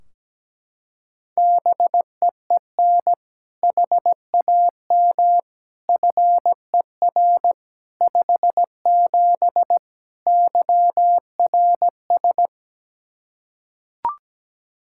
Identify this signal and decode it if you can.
Morse code, heard as a keyed tone, about 17 wpm, 700 Hz. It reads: BEEN HAM FER 57 YRS
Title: K1USN SST QSO Practice 17wpm